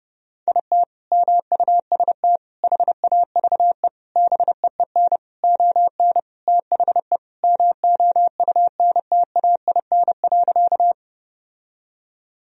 Morse code, heard as a keyed tone, about 30 wpm, 700 Hz. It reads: IT MUST HAVE BEEN ON THE MOUNTAIN.
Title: Sentences from Top 1000 Words - Review Part 1 - 30wpm